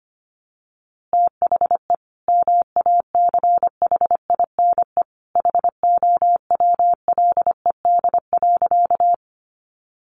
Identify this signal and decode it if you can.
Morse code, heard as a keyed tone, about 25 wpm, 700 Hz. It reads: THE MACHINE HOWLED.
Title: A Sound Of Thunder 25wpm